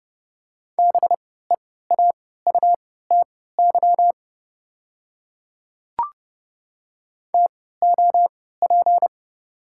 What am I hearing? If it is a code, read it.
Morse code, heard as a keyed tone, about 30 wpm, spaced as for 17 wpm, 700 Hz. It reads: BEAUTY TOP
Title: 3x Character Spacing - Top 500 Words - 30wpm